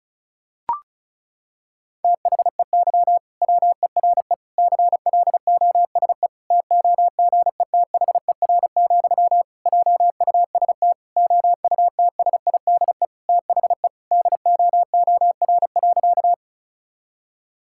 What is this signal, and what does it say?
Morse code, heard as a keyed tone, about 36 wpm, 700 Hz. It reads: THEY WERE CLOSE TOGETHER, JUST OUTSIDE THE DOOR.
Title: Sentences from Top 1000 Words - Review Part 6 - 35wpm